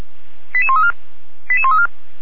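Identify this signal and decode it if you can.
Five-tone selective call (ZVEI), heard as quick runs of five tones, tones of about 70 ms. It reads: 88134 88134
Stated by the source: ZVEI five-tone selective call 88531 received on SDR.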